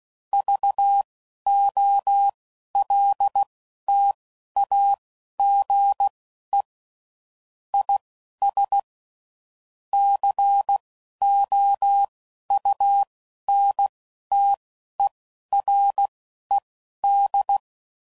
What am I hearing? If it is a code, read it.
Morse code, heard as a keyed tone, about 16 wpm, spaced as for 11 wpm, 800 Hz. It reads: VOLTAGE IS COUNTERED
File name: CW Code Groups @10WPM / Technical Subjects / XC